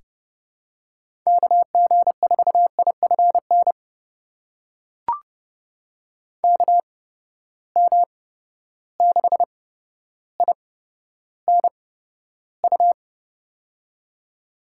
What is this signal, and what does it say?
Morse code, heard as a keyed tone, about 30 wpm, spaced as for 13 wpm, 700 Hz. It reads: KG4IFN KM6INU
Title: USA Call Signs - 6 Characters - 8x Character Spacing - 30wpm